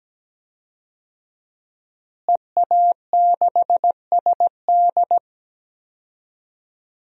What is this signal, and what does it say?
Morse code, heard as a keyed tone, about 17 wpm, 700 Hz. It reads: EA6SD
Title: DXCC-100 Call Signs - 5 Characters - Repeated 3x - 17wpm